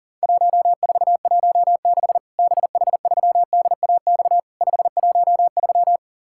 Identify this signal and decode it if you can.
Morse code, heard as a keyed tone, about 37 wpm, 700 Hz. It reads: 1416 BH3DAX 513
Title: Worked All Europe DX Contest - QTC Exchange - 40wpm